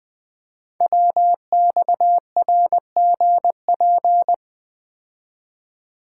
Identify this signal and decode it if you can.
Morse code, heard as a keyed tone, about 20 wpm, 700 Hz. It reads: WXRGP
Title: Code Groups - 5 Letters - 20wpm